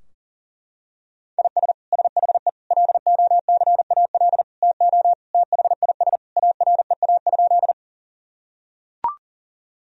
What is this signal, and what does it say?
Morse code, heard as a keyed tone, about 40 wpm, 700 Hz. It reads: IS SHE LOCAL TO THIS AREA?
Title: Sentences from Top 1000 Words - Review Part 3 - 40wpm